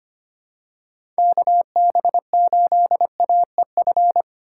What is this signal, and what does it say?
Morse code, heard as a keyed tone, about 25 wpm, 700 Hz. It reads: KB8AEF
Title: USA Call Signs - 6 Characters - 2x Character Spacing - 25wpm